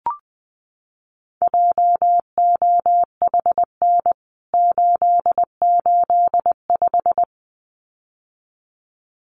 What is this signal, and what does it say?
Morse code, heard as a keyed tone, about 20 wpm, 700 Hz. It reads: JOHN 885
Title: CWT Exchange 20wpm